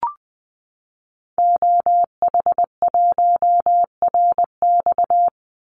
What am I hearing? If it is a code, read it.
Morse code, heard as a keyed tone, about 20 wpm, 700 Hz. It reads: OH1RX
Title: Call Signs - 20wpm